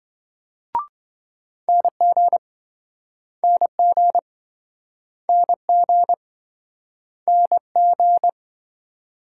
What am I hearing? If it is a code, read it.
Morse code, heard as a keyed tone, about 26 wpm, 700 Hz. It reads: NG NG NG NG